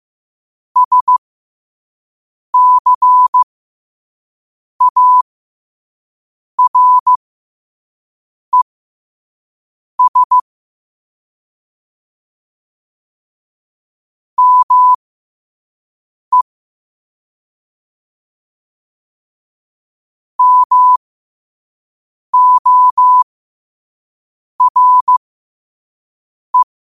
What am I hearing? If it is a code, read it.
Morse code, heard as a keyed tone, about 15 wpm, spaced as for 5 wpm, 1000 Hz. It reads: SCARES ME MORE
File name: qotdfiles / 5wpm